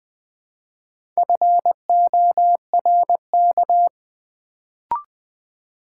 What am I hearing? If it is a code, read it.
Morse code, heard as a keyed tone, about 20 wpm, 700 Hz. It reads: FORK
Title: Build A Word - 2 Letters At A Time - 4-Letter Words - 20wpm